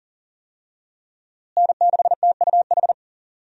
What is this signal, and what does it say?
Morse code, heard as a keyed tone, about 39 wpm, 700 Hz. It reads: N6TUH
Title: USA Call Signs - 5 Characters - Repeated 3x - 40wpm